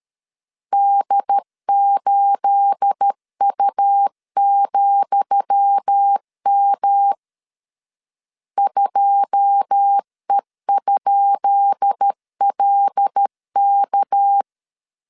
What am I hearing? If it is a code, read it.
Morse code, heard as a keyed tone, about 13 wpm, 800 Hz. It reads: D8U,M 2E?LK
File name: LEZIONI / 10 SIGNS / 10D COMA 12wpm 2544